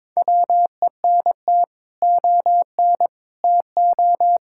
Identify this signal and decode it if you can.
Morse code, heard as a keyed tone, about 22 wpm, 700 Hz. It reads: WENT ON TO
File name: practice / 22wpm / Sets of 7 Words - 22wpm